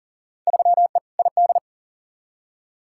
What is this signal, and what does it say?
Morse code, heard as a keyed tone, about 40 wpm, 700 Hz. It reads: 3E ID
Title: Field Day Exchanges Encore - 40wpm